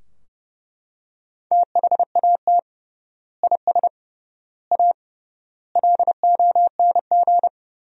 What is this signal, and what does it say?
Morse code, heard as a keyed tone, about 30 wpm, 700 Hz. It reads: THAT IS A LONG
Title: 3x Word-Spacing - Sentences from Top 100 Words - 30wpm